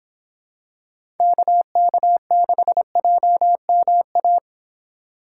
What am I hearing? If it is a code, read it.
Morse code, heard as a keyed tone, about 26 wpm, 700 Hz. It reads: KK6JMA